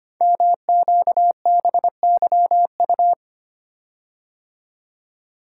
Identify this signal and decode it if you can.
Morse code, heard as a keyed tone, about 25 wpm, 700 Hz. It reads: MQBYU
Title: Code Groups - 5 Letters - 25wpm